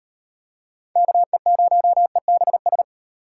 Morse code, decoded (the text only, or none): KE0EBS